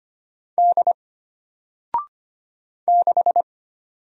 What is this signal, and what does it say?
Morse code, heard as a keyed tone, about 25 wpm, 700 Hz. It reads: D 6
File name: Single-Letter-Number-25wpm